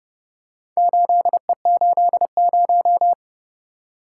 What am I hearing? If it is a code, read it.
Morse code, heard as a keyed tone, about 28 wpm, 700 Hz. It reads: 8E80